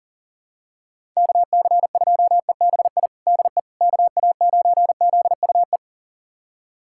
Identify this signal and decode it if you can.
Morse code, heard as a keyed tone, about 40 wpm, 700 Hz. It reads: KC2EBI DE KA9ZUE
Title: USA Callsign DE Callsign 40wpm